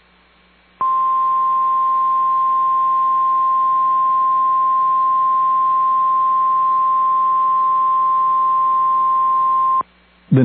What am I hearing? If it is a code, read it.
Emergency Alert System (SAME) tones (the 1050 Hz attention tone). It is heard from 0.8 to 9.8 s.